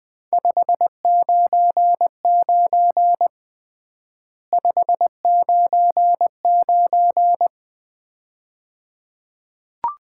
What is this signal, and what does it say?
Morse code, heard as a keyed tone, about 20 wpm, 700 Hz. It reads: 599 599
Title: Common QSO Phrases - 3x Word-Spacing - 20wpm